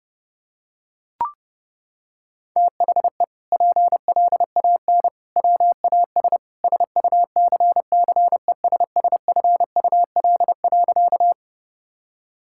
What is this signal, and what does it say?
Morse code, heard as a keyed tone, about 30 wpm, 700 Hz. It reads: THE PLAN WAS SUCCESSFUL.
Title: Sentences from Top 1000 Words - Review Part 6 - 30wpm